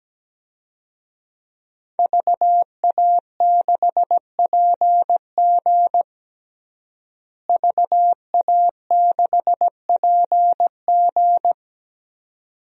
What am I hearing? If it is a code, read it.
Morse code, heard as a keyed tone, about 17 wpm, 700 Hz. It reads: VA6PG VA6PG
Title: DXCC-100 Call Signs - 5 Characters - Repeated 2x - 17wpm